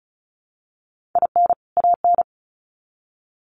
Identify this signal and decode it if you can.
Morse code, heard as a keyed tone, about 34 wpm, 700 Hz. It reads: IN AN
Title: Sets of 2 Words 35wpm